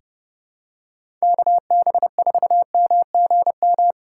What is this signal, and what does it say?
Morse code, heard as a keyed tone, about 30 wpm, 700 Hz. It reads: KB4MGM
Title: USA Call Signs - 6 Characters - 2x Character Spacing - 30wpm